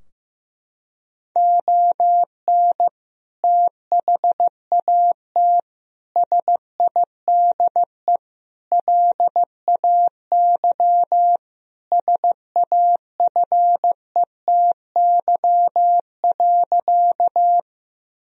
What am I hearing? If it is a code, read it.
Morse code, heard as a keyed tone, about 15 wpm, 700 Hz. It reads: ON THAT SIDE LAY SAFETY.
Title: Sentences from Top 1000 Words - Review Part 4 - 15wpm